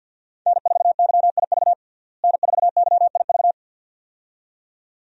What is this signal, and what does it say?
Morse code, heard as a keyed tone, about 51 wpm, 700 Hz. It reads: N4YIV N4YIV
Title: USA Call Signs - 5 Characters - Repeated 2x - 50wpm